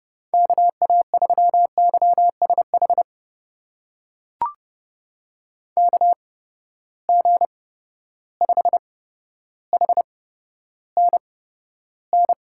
Morse code, heard as a keyed tone, about 30 wpm, spaced as for 13 wpm, 700 Hz. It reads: KA3YSH KG5HNN
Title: USA Call Signs - 6 Characters - 8x Character Spacing - 30wpm